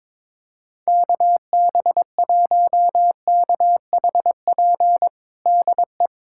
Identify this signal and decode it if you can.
Morse code, heard as a keyed tone, about 22 wpm, 700 Hz. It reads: KB1KHP DE
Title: USA Callsign DE Callsign 22wpm